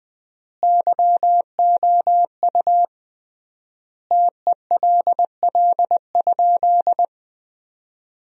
Morse code, heard as a keyed tone, about 20 wpm, 700 Hz. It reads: YOU TELL?
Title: 3x Word-Spacing - Sentences from Top 300 Words - 20wpm